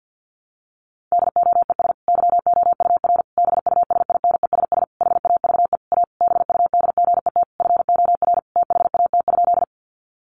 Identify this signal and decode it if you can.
Morse code, heard as a keyed tone, about 50 wpm, 700 Hz. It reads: DOES YOUR BUSINESS HAVE A BUDGET FOR THAT?
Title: Sentences from Top 1200 Words 50wpm